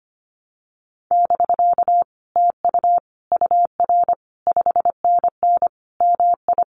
Morse code, heard as a keyed tone, about 25 wpm, 700 Hz. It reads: <BK> TU UR 5NN MI
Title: Common QSO Phrases Encore 25wpm